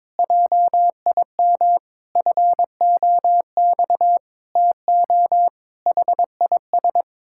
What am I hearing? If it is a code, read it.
Morse code, heard as a keyed tone, about 22 wpm, 700 Hz. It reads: JIM FOX TO HIS
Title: Pangram Challenge v2 - 22wpm